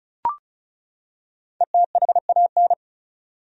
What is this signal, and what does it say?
Morse code, heard as a keyed tone, about 34 wpm, 700 Hz. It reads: ETHAN
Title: USA Common Names - 35wpm